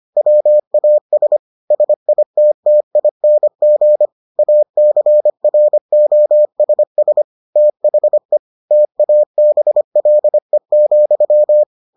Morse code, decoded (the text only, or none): WAS SITTING ACROSS THE TABLE,